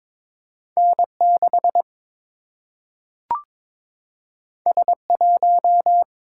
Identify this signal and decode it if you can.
Morse code, heard as a keyed tone, about 22 wpm, 700 Hz. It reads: N6 S1
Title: Single-Letter Single-Number 22wpm